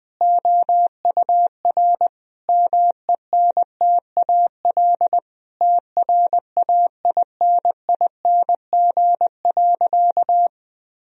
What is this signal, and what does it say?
Morse code, heard as a keyed tone, about 21 wpm, 700 Hz. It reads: OUR MENTAL TRAINING.
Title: Sentences from Top 1000 Words - Review Part 4 - 20wpm